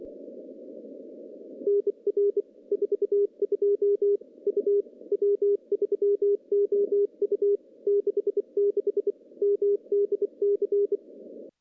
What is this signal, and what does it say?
Morse code, heard as a keyed tone, about 24 wpm, 410 Hz. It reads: NR 42 U W3OU 66 MDC
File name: contests / CW-SS-k5tr- -0100